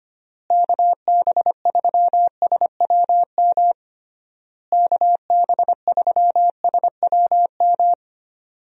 Morse code, heard as a keyed tone, about 25 wpm, 700 Hz. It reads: KB3SWM KB3SWM
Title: USA Call Signs - 6 Characters - Repeated 3x - 25wpm